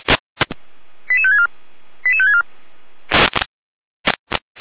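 Five-tone selective call (ZVEI), heard as quick runs of five tones, tones of about 80 ms. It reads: 88564 88564
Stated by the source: Five-tone ZVEI selective call 88505 received by SDR.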